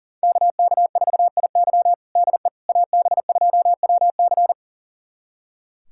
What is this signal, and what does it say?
Morse code, heard as a keyed tone, about 40 wpm, 700 Hz. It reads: KK4IY DE AB2WC
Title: USA Callsign DE Callsign - 5 Characters - 40wpm